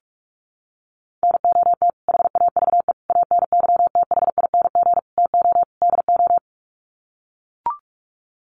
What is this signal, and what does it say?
Morse code, heard as a keyed tone, about 44 wpm, 700 Hz. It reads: NOT HAVE ANYTHING TO DO
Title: Sets of 5 Words Encore 45wpm